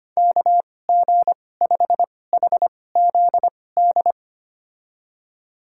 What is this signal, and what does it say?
Morse code, heard as a keyed tone, about 25 wpm, spaced as for 18 wpm, 700 Hz. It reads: KG5HZD
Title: USA Call Signs - 6 Characters - 2x Character Spacing - 25wpm